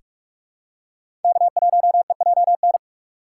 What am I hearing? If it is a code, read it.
Morse code, heard as a keyed tone, about 48 wpm, 700 Hz. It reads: K1EJN
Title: USA Call Signs - 5 Characters - 6x Character Spacing - 45wpm